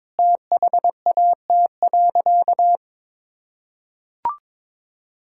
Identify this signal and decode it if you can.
Morse code, heard as a keyed tone, about 22 wpm, 700 Hz. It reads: THAT.
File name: practice / 22wpm / Sentences with 2-Word Phrases - 22wpm